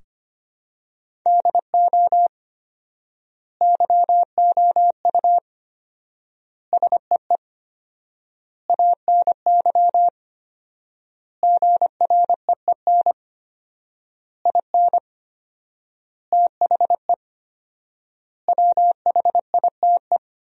Morse code, heard as a keyed tone, about 25 wpm, 700 Hz. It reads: DO YOU SEE ANY GREEN IN THE WHITE
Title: 4x Word-Spacing - Sentences from Top 500 Words - 25wpm